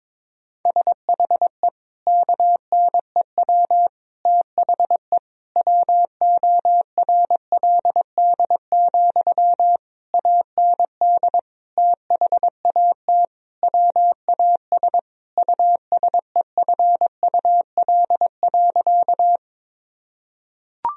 Morse code, heard as a keyed tone, about 22 wpm, 700 Hz. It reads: SHE KNEW THE WORLD, AND THAT WAS USEFUL.
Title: Sentences from Top 1000 Words - Review Part 4 - 22wpm